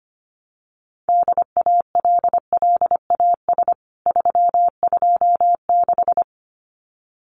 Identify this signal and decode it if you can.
Morse code, heard as a keyed tone, about 25 wpm, 700 Hz. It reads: DALLAS 326
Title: CWT Exchange 25wpm